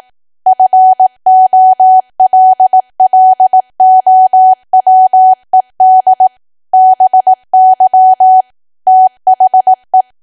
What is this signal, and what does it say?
Morse code, heard as a keyed tone, about 18 wpm, 750 Hz. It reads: FOLLOWED BY THE
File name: cwa / W1AW-Level2 / 18WPM